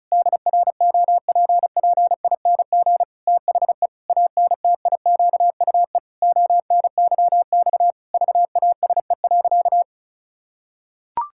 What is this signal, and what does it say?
Morse code, heard as a keyed tone, about 36 wpm, 700 Hz. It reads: DROPPING THE ANTIQUE ONYX VASE.
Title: Pangram Challenge v2 - 35wpm